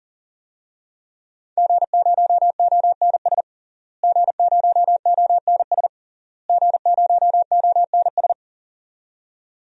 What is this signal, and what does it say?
Morse code, heard as a keyed tone, about 37 wpm, 700 Hz. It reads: G0ONS G0ONS G0ONS
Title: DXCC-100 Call Signs - 5 Characters - Repeated 3x - 40wpm